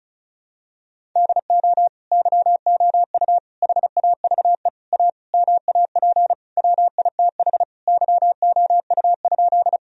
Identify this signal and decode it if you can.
Morse code, heard as a keyed tone, about 36 wpm, 700 Hz. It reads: DO YOU HAVE A MAP WITH YOU?
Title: Sentences from Top 1000 Words - Review Part 2 - 35wpm